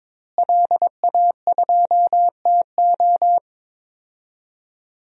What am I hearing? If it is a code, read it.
Morse code, heard as a keyed tone, about 22 wpm, 700 Hz. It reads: LA2TO
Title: DXCC-100 Call Signs - 5 Characters - Repeated 2x - 22wpm